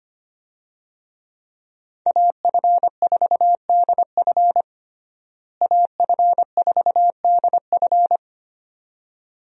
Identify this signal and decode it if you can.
Morse code, heard as a keyed tone, about 25 wpm, 700 Hz. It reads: AF4DF AF4DF
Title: USA Call Signs - 5 Characters - Repeated 2x - 25wpm